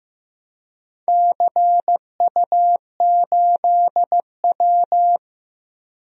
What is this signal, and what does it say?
Morse code, heard as a keyed tone, about 15 wpm, 700 Hz. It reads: CU8W
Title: DXCC-100 Call Signs - 4 Characters - Repeated 3x - 15wpm